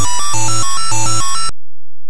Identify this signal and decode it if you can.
There is no code.